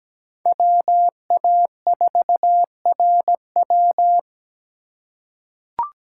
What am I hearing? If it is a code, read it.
Morse code, heard as a keyed tone, about 17 wpm, 700 Hz. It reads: WA4RW
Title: USA Call Signs - 5 Characters - 4x Character Spacing - 17wpm